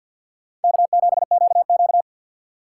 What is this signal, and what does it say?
Morse code, heard as a keyed tone, about 51 wpm, 700 Hz. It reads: K7QQ
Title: USA Call Signs - 4 Characters - 2x Character Spacing - 50wpm